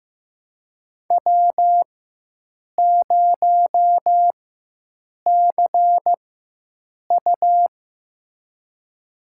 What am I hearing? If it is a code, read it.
Morse code, heard as a keyed tone, about 15 wpm, spaced as for 7 wpm, 700 Hz. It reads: W0CU
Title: USA Call Signs - 4 Characters - 4x Character Spacing - 15wpm